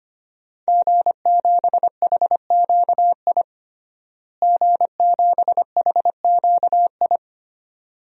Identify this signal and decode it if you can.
Morse code, heard as a keyed tone, about 25 wpm, 700 Hz. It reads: G7HQI G7HQI
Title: DXCC-100 Call Signs - 5 Characters - Repeated 2x - 25wpm